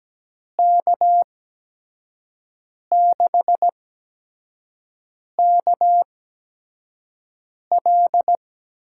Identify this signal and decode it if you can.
Morse code, heard as a keyed tone, about 17 wpm, spaced as for 5 wpm, 700 Hz. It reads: K6KL
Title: USA Call Signs - 8x Character Spacing - 17wpm